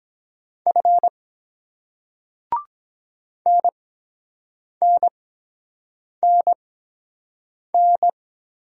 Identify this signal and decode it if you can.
Morse code, heard as a keyed tone, about 24 wpm, spaced as for 7 wpm, 700 Hz. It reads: F NNNN